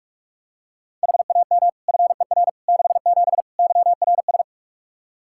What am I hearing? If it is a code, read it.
Morse code, heard as a keyed tone, about 46 wpm, 700 Hz. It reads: HAM FER 67 YRS
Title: Practice QSO - Exchange 3 - More Info - 45wpm